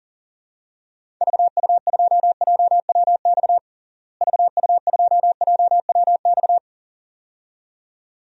Morse code, heard as a keyed tone, about 40 wpm, 700 Hz. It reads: VU2JWX VU2JWX
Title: DXCC-100 Call Signs - 6 Characters - Repeated 2x - 40wpm